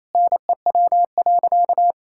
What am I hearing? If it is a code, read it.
Morse code, heard as a keyed tone, about 28 wpm, 700 Hz. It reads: NEW.
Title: Sentences from Top 1600 Words 28wpm